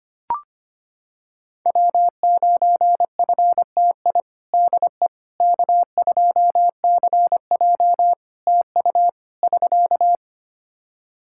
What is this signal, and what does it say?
Morse code, heard as a keyed tone, about 25 wpm, 700 Hz. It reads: W9FTI DE K2CJ TU <SK>
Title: K1USN SST QSO Practice 25wpm